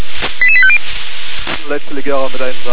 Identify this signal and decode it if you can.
Five-tone selective call (ZVEI), heard as quick runs of five tones, tones of about 70 ms, with speech. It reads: 88850